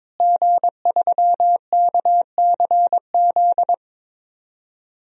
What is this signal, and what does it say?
Morse code, heard as a keyed tone, about 22 wpm, 700 Hz. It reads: G3KCZ
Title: DXCC-100 Call Signs - 5 Characters - Repeated 2x - 22wpm